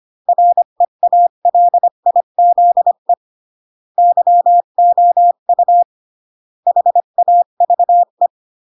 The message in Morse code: REALIZE YOU HAVE